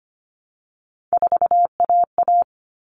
Morse code, heard as a keyed tone, about 25 wpm, 700 Hz. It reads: 4AA
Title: All Cut Numbers - 3-Digit 25wpm